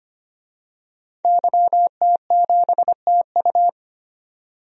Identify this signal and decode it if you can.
Morse code, heard as a keyed tone, about 25 wpm, 700 Hz. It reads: YT7TU
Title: DXCC-100 Call Signs - 5 Characters - 25wpm